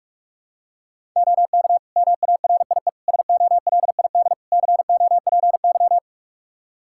Morse code, heard as a keyed tone, about 46 wpm, 700 Hz. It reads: OK MARIE SOLID COPY
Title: Practice QSO - Exchange 3 - More Info - 45wpm